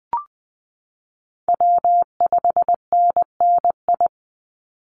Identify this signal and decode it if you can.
Morse code, heard as a keyed tone, about 20 wpm, 700 Hz. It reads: W5NNI
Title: USA Call Signs 20wpm